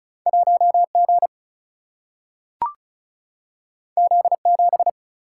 Morse code, heard as a keyed tone, about 35 wpm, 700 Hz. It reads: 1G Z7
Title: Single-Letter Single-Number 35wpm